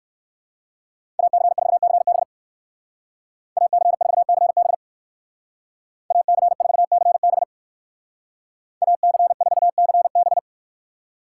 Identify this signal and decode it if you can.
Morse code, heard as a keyed tone, about 56 wpm, 700 Hz. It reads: AC4CB AC4CB AC4CB AC4CB